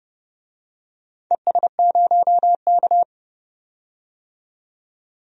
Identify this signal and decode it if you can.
Morse code, heard as a keyed tone, about 30 wpm, 700 Hz. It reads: ES0K
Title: DXCC-100 Call Signs - 4 Characters - 30wpm